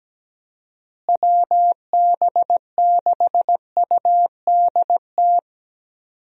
Morse code, heard as a keyed tone, about 17 wpm, 700 Hz. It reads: WB6UDT
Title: USA Call Signs - 6 Characters - Repeated 3x - 17wpm